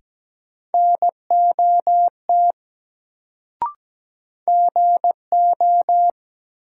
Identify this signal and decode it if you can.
Morse code, heard as a keyed tone, about 17 wpm, 700 Hz. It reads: NOT GO
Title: Sets of 2 Words - New Word Drill - 17wpm